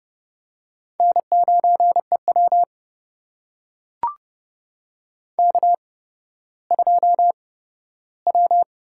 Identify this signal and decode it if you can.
Morse code, heard as a keyed tone, about 30 wpm, spaced as for 15 wpm, 700 Hz. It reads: N9EW K2W
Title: USA Call Signs - 4 Characters - 8x Character Spacing - 30wpm